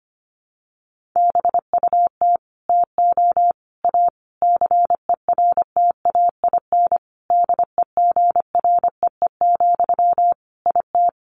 Morse code, heard as a keyed tone, about 25 wpm, 700 Hz. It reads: BUT TO A CERTAIN DEGREE, IT
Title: Sentences from Top 1500 Words 25wpm